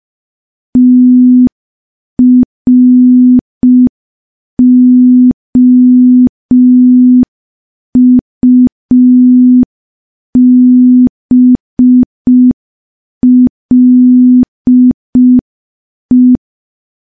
Morse code, heard as a keyed tone, about 5 wpm, 250 Hz. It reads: TROUBLE
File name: morse / morse-training-05WPM